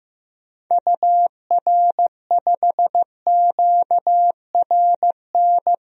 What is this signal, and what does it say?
Morse code, heard as a keyed tone, about 16 wpm, 700 Hz. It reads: UR5QRN
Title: DXCC-100 Call Signs - 6 Characters - 15wpm